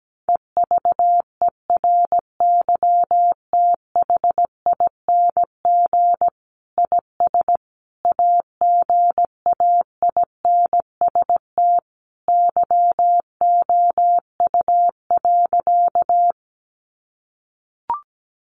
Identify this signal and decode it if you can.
Morse code, heard as a keyed tone, about 17 wpm, 700 Hz. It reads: EVERYTHING IS AGAINST YOU.
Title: Sentences from Top 700 Words 17wpm